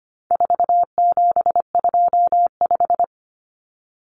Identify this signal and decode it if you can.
Morse code, heard as a keyed tone, about 25 wpm, 700 Hz. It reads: 4725
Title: Numbers - 4-Digit 25wpm